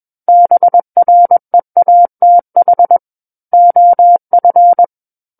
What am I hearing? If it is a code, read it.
Morse code, heard as a keyed tone, about 21 wpm, 700 Hz. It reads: BREATH OF
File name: Book One / 12-Chapter 12 - 21 WPM